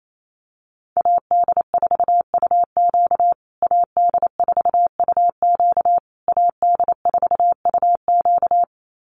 Morse code, heard as a keyed tone, about 28 wpm, 700 Hz. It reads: AD4UQ AD4UQ AD4UQ
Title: USA Call Signs Repeated 3x 28wpm